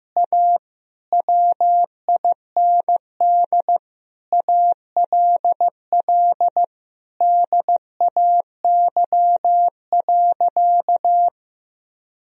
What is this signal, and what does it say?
Morse code, heard as a keyed tone, about 15 wpm, 700 Hz. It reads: A WIND ALL DAY.
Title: Sentences from Top 1000 Words - Review Part 2 - 15wpm